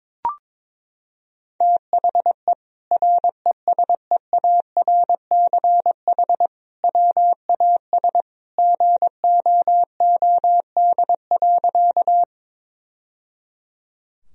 Morse code, heard as a keyed tone, about 22 wpm, 700 Hz. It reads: THE RESEARCH WAS GOOD.
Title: Sentences from Top 1000 Words - Review Part 4 - 22wpm